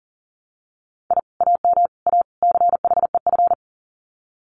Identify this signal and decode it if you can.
Morse code, heard as a keyed tone, about 40 wpm, 700 Hz. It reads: I AM A CHEF
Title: Common QSO Phrases Encore 40wpm